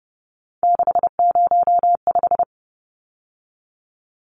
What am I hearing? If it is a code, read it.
Morse code, heard as a keyed tone, about 30 wpm, 700 Hz. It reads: 605
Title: Numbers - 3-Digit 30wpm